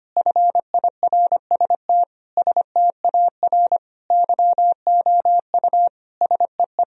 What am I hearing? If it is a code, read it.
Morse code, heard as a keyed tone, about 25 wpm, 700 Hz. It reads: FIRST STAR YOU SEE
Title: Sentences from Top 1000 Words - Review Part 2 - 25wpm